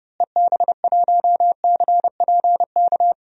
Morse code, heard as a keyed tone, about 32 wpm, 700 Hz. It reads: EB1CPK
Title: DXCC-100 Call Signs - 6 Characters - Repeated 3x - 30wpm